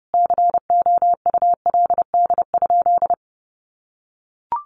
Morse code, heard as a keyed tone, about 30 wpm, 700 Hz. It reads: COULD?
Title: Sentences from Top 400 Words 30wpm